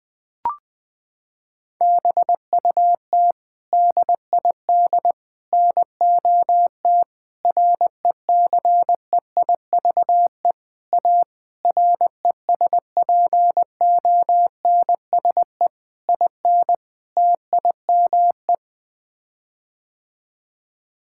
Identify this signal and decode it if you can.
Morse code, heard as a keyed tone, about 20 wpm, 700 Hz. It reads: BUT DID NOT RECEIVE A RESPONSE IN TIME
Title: Sets of 8 Words - 20wpm